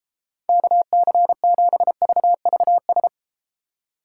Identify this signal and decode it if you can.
Morse code, heard as a keyed tone, about 33 wpm, 700 Hz. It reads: KC7VVS